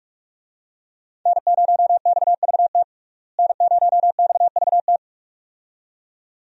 Morse code, heard as a keyed tone, about 46 wpm, 700 Hz. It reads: N0XVT N0XVT
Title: USA Call Signs - 5 Characters - Repeated 2x - 45wpm